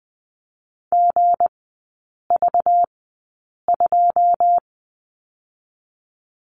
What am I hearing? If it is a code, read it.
Morse code, heard as a keyed tone, about 20 wpm, 700 Hz. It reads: G V 2
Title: 198 Lesson 32e - Letter k - Three Characters 20wpm